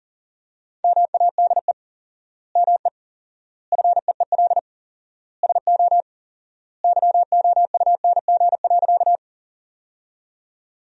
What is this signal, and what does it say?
Morse code, heard as a keyed tone, about 40 wpm, 700 Hz. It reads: MADE ME FEEL SO YOUNG.
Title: 4x Word-Spacing - Sentences from Top 400 Words - 40wpm